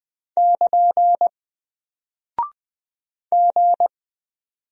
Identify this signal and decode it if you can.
Morse code, heard as a keyed tone, about 20 wpm, 700 Hz. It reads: <KN> G
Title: Single Character (Letter:Number:Punctuation:Prosign) Alt 20wpm